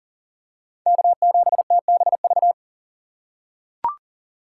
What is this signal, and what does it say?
Morse code, heard as a keyed tone, about 40 wpm, 700 Hz. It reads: K7TBV
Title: USA Call Signs - 5 Characters - 4x Character Spacing - 40wpm